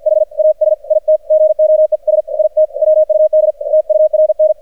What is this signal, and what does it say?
Morse code, heard as a keyed tone, about 51 wpm, spaced as for 45 wpm, 620 Hz, with radio noise. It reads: HAS IT GOES IT WUD TURN
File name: archived / CRSnet